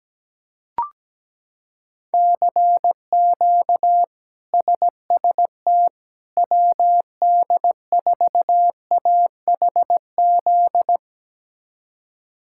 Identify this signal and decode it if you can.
Morse code, heard as a keyed tone, about 17 wpm, 700 Hz. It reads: CQ SST WD4AHZ
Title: K1USN SST QSO Practice - 17wpm